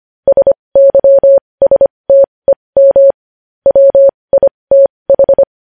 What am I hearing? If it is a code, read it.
Morse code, heard as a keyed tone, about 25 wpm, spaced as for 20 wpm, 550 Hz. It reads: SYSTEM WITH
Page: news-headlines-in-morse-code-at-20-wpm